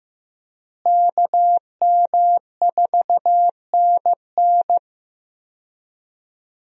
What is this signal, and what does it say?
Morse code, heard as a keyed tone, about 15 wpm, 700 Hz. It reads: KM4NN
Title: USA Call Signs - 5 Characters - Repeated 2x - 15wpm